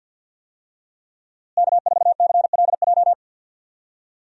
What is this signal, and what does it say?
Morse code, heard as a keyed tone, about 48 wpm, 700 Hz. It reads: K4CLJ
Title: USA Call Signs - 5 Characters - Repeated 3x - 50wpm